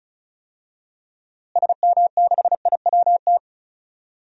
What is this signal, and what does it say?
Morse code, heard as a keyed tone, about 36 wpm, 700 Hz. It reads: SM6IWT
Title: DXCC-100 Call Signs - 6 Characters - Repeated 2x - 35wpm